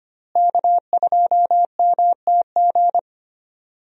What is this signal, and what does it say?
Morse code, heard as a keyed tone, about 25 wpm, 700 Hz. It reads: K2MTG
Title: USA Call Signs - 5 Characters - Repeated 3x - 25wpm